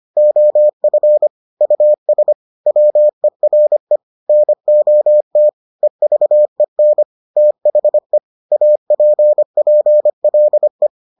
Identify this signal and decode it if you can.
Morse code, heard as a keyed tone, about 25 wpm, 600 Hz. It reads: OF US WERE NOT EVEN THE APPLE